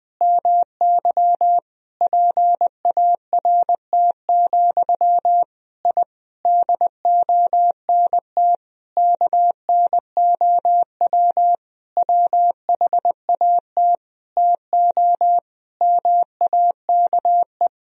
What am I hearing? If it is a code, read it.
Morse code, heard as a keyed tone, about 20 wpm, 700 Hz. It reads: MY PART, I DONT KNOW WHAT TO MAKE
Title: Sentences from Top 1000 Words - Review Part 1 - 20wpm